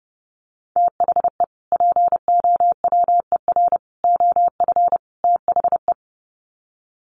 Morse code, heard as a keyed tone, about 30 wpm, 700 Hz. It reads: THE POWER OF THE
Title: Sets of 4 Words 30wpm